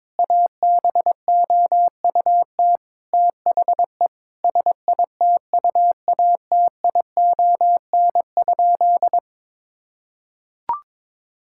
Morse code, heard as a keyed tone, about 22 wpm, 700 Hz. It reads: ABOUT THE SITUATION?
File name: practice / 22wpm / Sentences with 8-Word Phrases - 22wpm